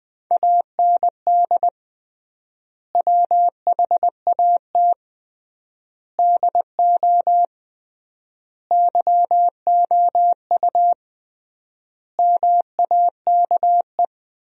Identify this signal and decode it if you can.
Morse code, heard as a keyed tone, about 21 wpm, 700 Hz. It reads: AND WHAT DO YOU MAKE
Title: 3x Word-Spacing - Sentences from Top 100 Words - 20wpm